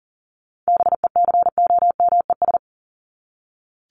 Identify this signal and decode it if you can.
Morse code, heard as a keyed tone, about 40 wpm, 700 Hz. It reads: BECOMES
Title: ES:S Suffix - Top 100 Words with Base Words 40wpm